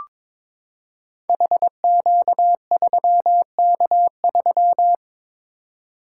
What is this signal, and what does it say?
Morse code, heard as a keyed tone, about 22 wpm, 700 Hz. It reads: HQ3K3
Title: Code Groups - 5 Letter:Numbers - 22wpm